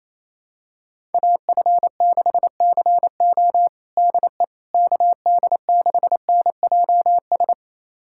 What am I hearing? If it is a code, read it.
Morse code, heard as a keyed tone, about 28 wpm, 700 Hz. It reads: AF6CO DE KD6NJS
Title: USA Callsign DE Callsign 28wpm